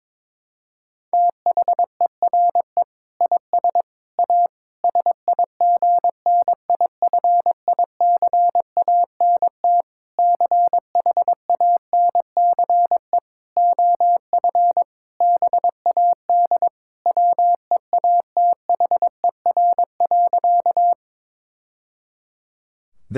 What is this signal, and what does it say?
Morse code, heard as a keyed tone, about 22 wpm, 700 Hz. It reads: THERE IS A SIGNIFICANT CHANCE OF BAD WEATHER.
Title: Sentences from Top 1000 Words - Review Part 6 - 22wpm